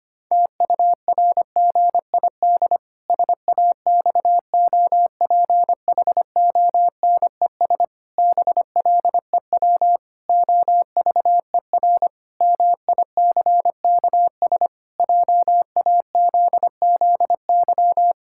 Morse code, heard as a keyed tone, about 25 wpm, 700 Hz. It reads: TURGID SAXOPHONES BLEW OVER MICKS JAZZY
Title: Pangram Challenge v2 - 25wpm